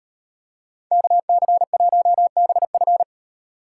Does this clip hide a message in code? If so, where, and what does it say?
Yes: Morse code from 0.9 to 3.0 s, heard as a keyed tone, about 33 wpm, 700 Hz. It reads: KC1BF